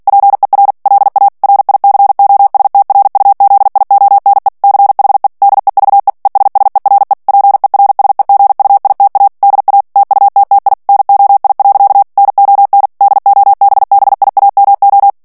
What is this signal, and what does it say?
Morse code, heard as a keyed tone, about 48 wpm, 800 Hz. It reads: PER LA RICOSTRUZIONE CHE DEVE ESSERE PERSEGUITA DA TUTTI NOI. NON DOBBIAMO
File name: morse mp3 / ITALIEN texte clair 1 vitesse 45 mots minute